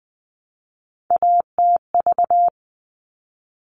Morse code, heard as a keyed tone, about 20 wpm, 700 Hz. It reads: ATV
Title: All Cut Numbers - 3-Digit 20wpm